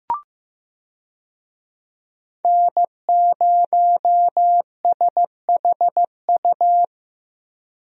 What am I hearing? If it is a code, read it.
Morse code, heard as a keyed tone, about 15 wpm, 700 Hz. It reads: N0SHU
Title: USA Call Signs - 5 Characters - Repeated 2x - 15wpm